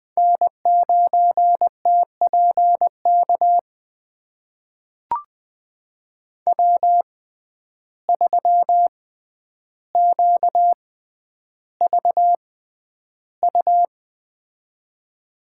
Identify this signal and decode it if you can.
Morse code, heard as a keyed tone, about 20 wpm, spaced as for 10 wpm, 700 Hz. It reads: N9TPK W3QVU
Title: USA Call Signs - 5 Characters - 6x Character Spacing - 20wpm